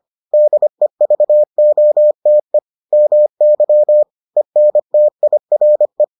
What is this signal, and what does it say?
Morse code, heard as a keyed tone, about 25 wpm, 600 Hz. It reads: DEVOTE MY ENTIRE